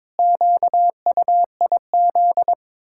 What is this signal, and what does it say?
Morse code, heard as a keyed tone, about 22 wpm, 700 Hz. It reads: QUIZ
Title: Pangram Challenge v2 - 22wpm